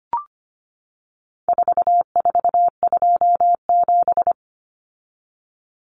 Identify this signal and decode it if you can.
Morse code, heard as a keyed tone, about 25 wpm, 700 Hz. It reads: 4427
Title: Numbers - 4-Digit 25wpm